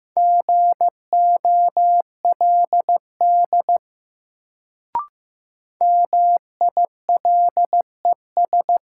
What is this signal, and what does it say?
Morse code, heard as a keyed tone, about 15 wpm, 700 Hz. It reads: GOLD MILES
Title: Sentences from Top 1000 Words - Review Part 3 - New Word Drill - 15wpm